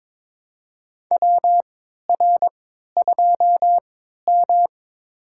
Morse code, heard as a keyed tone, about 22 wpm, spaced as for 13 wpm, 700 Hz. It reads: WR2M
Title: USA Call Signs - 4 Characters - 3x Character Spacing - 22wpm